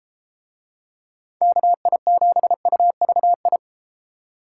Morse code, heard as a keyed tone, about 32 wpm, 700 Hz. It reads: KI7UVI